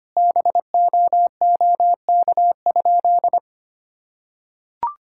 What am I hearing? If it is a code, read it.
Morse code, heard as a keyed tone, about 25 wpm, 700 Hz. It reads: BOOK?
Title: Sentences from Top 1000 Words - Review Part 5 - 25wpm